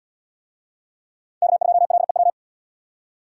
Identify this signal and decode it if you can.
Morse code, heard as a keyed tone, about 71 wpm, 700 Hz. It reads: K2LEW